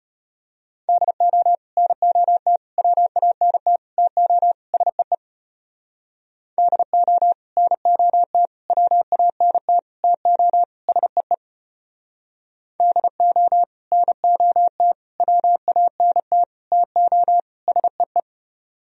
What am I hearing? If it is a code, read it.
Morse code, heard as a keyed tone, about 33 wpm, 700 Hz. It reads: DO NOT WANT TO SEE DO NOT WANT TO SEE DO NOT WANT TO SEE